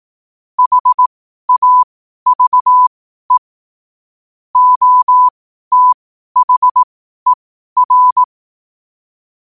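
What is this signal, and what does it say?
Morse code, heard as a keyed tone, about 18 wpm, spaced as for 12 wpm, 1000 Hz. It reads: HAVE OTHER
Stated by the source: Morse code sent at 12 words per minute.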